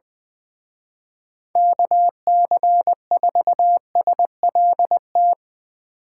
Morse code, heard as a keyed tone, about 23 wpm, 700 Hz. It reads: KC4SLT
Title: Speed Racing - USA 2x3 Call Signs 20wpm